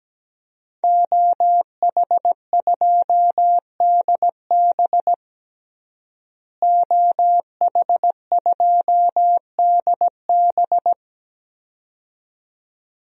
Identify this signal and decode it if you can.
Morse code, heard as a keyed tone, about 17 wpm, 700 Hz. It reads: OH2DB OH2DB
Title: DXCC-100 Call Signs - 5 Characters - Repeated 2x - 17wpm